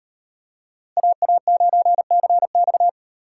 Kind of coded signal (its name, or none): Morse code